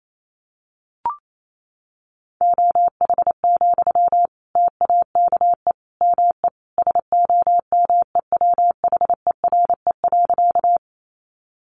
Morse code, heard as a keyed tone, about 28 wpm, 700 Hz. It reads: OH, TAKE ME SOMEWHERE.
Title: Sentences from Top 1400 Words 28wpm